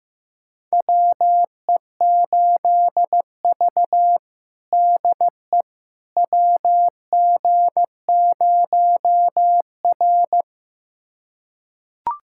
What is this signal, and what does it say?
Morse code, heard as a keyed tone, about 15 wpm, 700 Hz. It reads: WE8V DE WG0R
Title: USA Callsign DE Callsign - 4 Characters - 15wpm